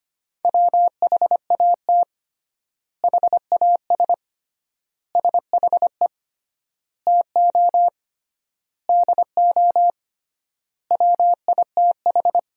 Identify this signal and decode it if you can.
Morse code, heard as a keyed tone, about 25 wpm, 700 Hz. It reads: WHAT HAS SHE TO DO WITH
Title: 3x Word-Spacing - Sentences from Top 100 Words - 25wpm